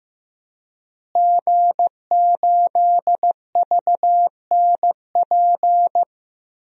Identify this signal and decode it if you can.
Morse code, heard as a keyed tone, about 15 wpm, 700 Hz. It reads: G8VNP
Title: DXCC-100 Call Signs - 5 Characters - Repeated 3x - 15wpm